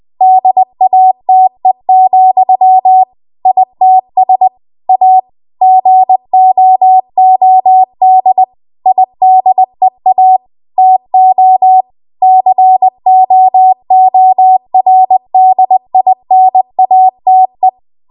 Morse code, heard as a keyed tone, about 20 wpm, 750 Hz. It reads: DATE, ITS A GOOD IDEA TO COORDINATE